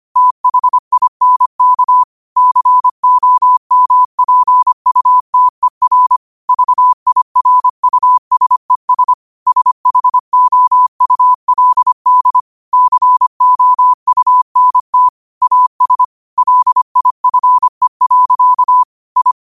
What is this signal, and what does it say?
Morse code, heard as a keyed tone, about 25 wpm, 1000 Hz. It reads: THINK COMPUTER VIRUSES SHOULD COUNT AS LIFE. I
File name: qotdfiles / 25wpm